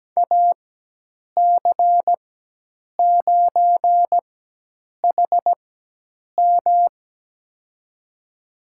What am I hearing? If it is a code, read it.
Morse code, heard as a keyed tone, about 17 wpm, spaced as for 8 wpm, 700 Hz. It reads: AC9HM
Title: USA Call Signs - 5 Characters - 4x Character Spacing - 17wpm